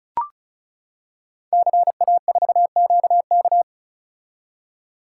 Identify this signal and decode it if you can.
Morse code, heard as a keyed tone, about 36 wpm, 700 Hz. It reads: CA4QK
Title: Code Groups - 5 Letter:Numbers - 35wpm